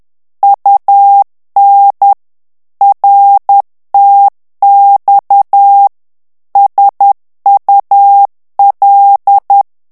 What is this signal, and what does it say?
Morse code, heard as a keyed tone, about 10 wpm, 800 Hz. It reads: UN RTX SUL
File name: morse mp3 / ITALIEN texte clair 1 vitesse 10 mots minute